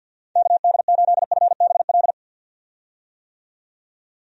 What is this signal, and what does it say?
Morse code, heard as a keyed tone, about 48 wpm, 700 Hz. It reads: KD8FBL
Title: USA Call Signs - 6 Characters - 50wpm